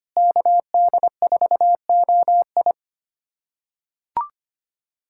Morse code, heard as a keyed tone, about 25 wpm, 700 Hz. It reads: KD4OI
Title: USA Callsign DE Callsign 25wpm